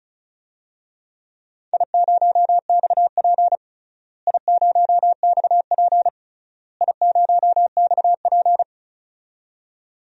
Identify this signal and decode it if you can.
Morse code, heard as a keyed tone, about 34 wpm, 700 Hz. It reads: I0XP I0XP I0XP
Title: DXCC-100 Call Signs - 4 Characters - Repeated 3x - 35wpm